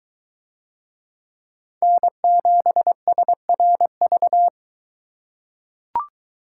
Morse code, heard as a keyed tone, about 23 wpm, 700 Hz. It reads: N7SRV